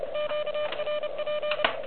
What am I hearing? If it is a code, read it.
Morse code, heard as a keyed tone, about 24 wpm, 600 Hz, with noise. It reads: QRP
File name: OK QRP 2022 / Zvuk / REC31